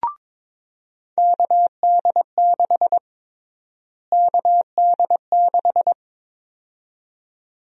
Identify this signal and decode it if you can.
Morse code, heard as a keyed tone, about 22 wpm, 700 Hz. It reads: KD6 KD6
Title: USA Call Sign Prefixes - Repeated 2x - 22wpm